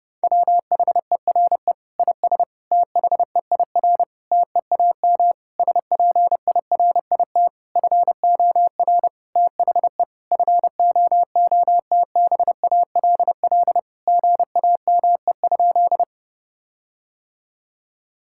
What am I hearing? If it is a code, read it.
Morse code, heard as a keyed tone, about 30 wpm, 700 Hz. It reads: WHERE IS THEIR TEAM SPIRIT FOR THE FOOTBALL GAME?
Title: Sentences from Top 1600 Words 30wpm